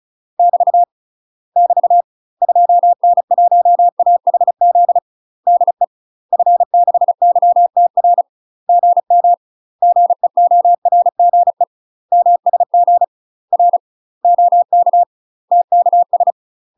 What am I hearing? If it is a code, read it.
Morse code, heard as a keyed tone, about 34 wpm, 700 Hz. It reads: <BT> <BT> 2N1AHZ DE F6YTR GM GEORGE MSG R OK TKS
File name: UK MASTER QSOs TRAINING / 35WPM / UK MASTER QSOs 0002